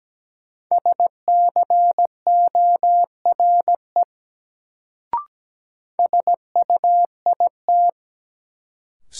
Morse code, heard as a keyed tone, about 17 wpm, 700 Hz. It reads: SCORE SUIT
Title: Sentences from Top 1600 Words - New Word Drill Alt - 17wpm